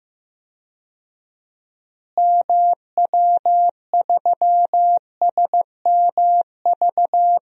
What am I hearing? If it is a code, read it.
Morse code, heard as a keyed tone, about 15 wpm, 700 Hz. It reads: MW3SMV
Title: DXCC-100 Call Signs - 6 Characters - Repeated 2x - 15wpm